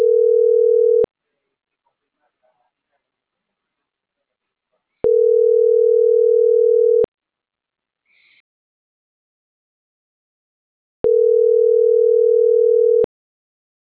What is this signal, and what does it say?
Ringback tone (North America), 440 + 480 Hz.